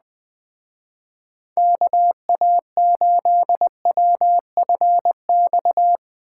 Morse code, heard as a keyed tone, about 20 wpm, 700 Hz. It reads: KA8WFX
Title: Speed Racing - USA 2x3 Call Signs 17wpm